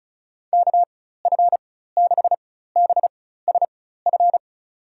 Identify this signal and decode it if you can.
Morse code, heard as a keyed tone, about 36 wpm, spaced as for 16 wpm, 700 Hz. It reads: KF6BSF
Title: USA Call Signs - 6 Characters - 4x Character Spacing - 35wpm